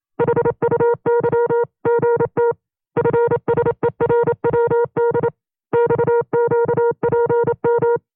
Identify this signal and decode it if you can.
Morse code, heard as a keyed tone, about 27 wpm, 470 Hz. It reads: HUYGT FSERWD XQPM
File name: 25WPM